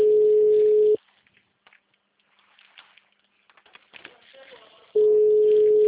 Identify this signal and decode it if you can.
Ringback tone (Europe, 425 Hz), with speech.